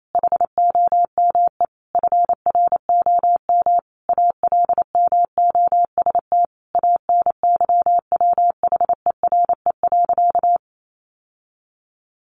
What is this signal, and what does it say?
Morse code, heard as a keyed tone, about 28 wpm, 700 Hz. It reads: HOME FROM ALMOST ANYWHERE.
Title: Sentences from Top 1400 Words 28wpm